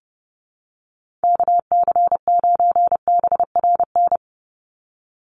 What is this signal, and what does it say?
Morse code, heard as a keyed tone, about 30 wpm, 700 Hz. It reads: KC9BRN
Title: USA Call Signs 30wpm